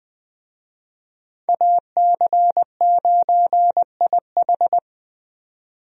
Morse code, heard as a keyed tone, about 20 wpm, 700 Hz. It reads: AC9IH
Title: USA Call Signs - 5 Characters - 20wpm